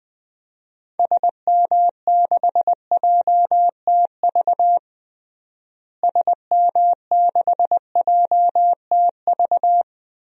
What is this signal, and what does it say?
Morse code, heard as a keyed tone, about 20 wpm, 700 Hz. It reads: SM6JTV SM6JTV
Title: DXCC-100 Call Signs - 6 Characters - Repeated 2x - 20wpm